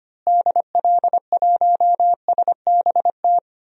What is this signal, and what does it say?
Morse code, heard as a keyed tone, about 25 wpm, 700 Hz. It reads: DL1SBT
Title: DXCC-100 Call Signs - 6 Characters - Repeated 3x - 25wpm